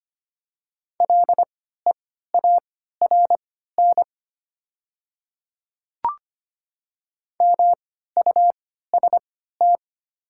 Morse code, heard as a keyed tone, about 25 wpm, spaced as for 14 wpm, 700 Hz. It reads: LEARN MUST
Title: 3x Character Spacing - Top 300 Words - 25wpm